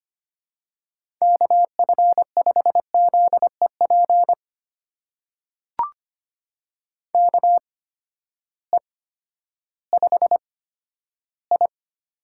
Morse code, heard as a keyed tone, about 25 wpm, spaced as for 13 wpm, 700 Hz. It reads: KF5ZEP KE5I
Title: USA Call Signs - 6 Characters - 8x Character Spacing - 25wpm